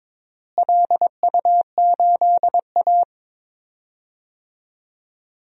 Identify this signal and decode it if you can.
Morse code, heard as a keyed tone, about 22 wpm, 700 Hz. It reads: LU8A
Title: DXCC-100 Call Signs - 4 Characters - Repeated 3x - 22wpm